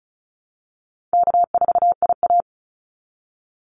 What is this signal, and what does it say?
Morse code, heard as a keyed tone, about 34 wpm, 700 Hz. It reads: K4IA
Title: Easy Call Signs Repeated 2x - 35wpm